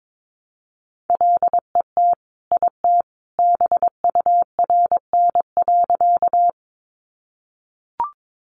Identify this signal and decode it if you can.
Morse code, heard as a keyed tone, about 22 wpm, 700 Hz. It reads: LET IT BURN.
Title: Sentences from Top 1300 Words 22wpm